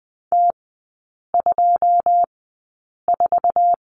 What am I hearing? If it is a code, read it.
Morse code, heard as a keyed tone, about 20 wpm, spaced as for 8 wpm, 700 Hz. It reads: T24
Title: 127 Lesson 21e - Number 6 - Three Characters 20wpm